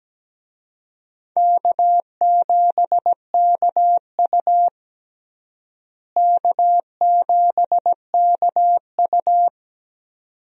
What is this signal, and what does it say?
Morse code, heard as a keyed tone, about 17 wpm, 700 Hz. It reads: K7KU K7KU
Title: USA Call Signs - 4 Characters - Repeated 2x - 17wpm